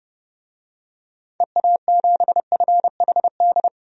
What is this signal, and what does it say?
Morse code, heard as a keyed tone, about 31 wpm, 700 Hz. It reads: EA7FHD
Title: DXCC-100 Call Signs - 6 Characters - 30wpm